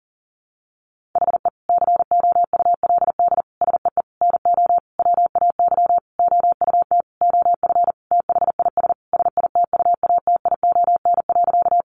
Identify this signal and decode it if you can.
Morse code, heard as a keyed tone, about 40 wpm, 700 Hz. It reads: HE COULD SEE NO WAY OUT OF THIS SITUATION.
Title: Sentences from Top 1200 Words 40wpm